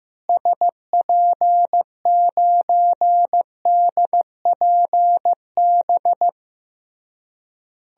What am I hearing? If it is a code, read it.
Morse code, heard as a keyed tone, about 15 wpm, 700 Hz. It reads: SP9DPB
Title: DXCC-100 Call Signs - 6 Characters - Repeated 3x - 15wpm